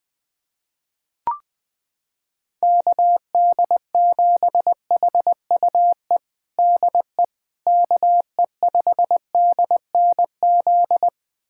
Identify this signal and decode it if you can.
Morse code, heard as a keyed tone, about 20 wpm, 700 Hz. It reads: KD7HUE DE KE5DNZ
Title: USA Callsign DE Callsign - 6 Characters - 20wpm